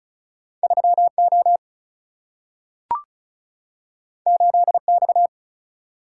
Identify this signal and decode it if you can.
Morse code, heard as a keyed tone, about 36 wpm, 700 Hz. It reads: 3O 8X
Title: Single-Letter Single-Number 35wpm